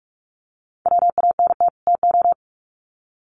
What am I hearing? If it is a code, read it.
Morse code, heard as a keyed tone, about 44 wpm, 700 Hz. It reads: WANT TO
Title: Sets of 2 Words 45wpm